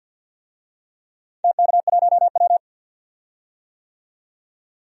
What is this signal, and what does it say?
Morse code, heard as a keyed tone, about 51 wpm, 700 Hz. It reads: TK1W
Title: DXCC-100 Call Signs - 4 Characters - 50wpm